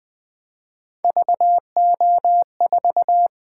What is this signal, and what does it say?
Morse code, heard as a keyed tone, about 20 wpm, 700 Hz. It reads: VO4
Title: Three Letter-Number Combinations - 20wpm